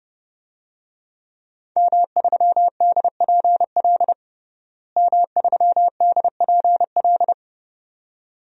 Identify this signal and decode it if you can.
Morse code, heard as a keyed tone, about 30 wpm, 700 Hz. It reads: M3DPL M3DPL
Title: DXCC-100 Call Signs - 5 Characters - Repeated 2x - 30wpm